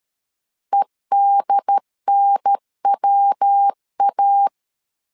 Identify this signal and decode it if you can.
Morse code, heard as a keyed tone, about 13 wpm, 800 Hz. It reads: EDNWA
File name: LEZIONI / 3 MWRO / 3 MW 12wpm 1633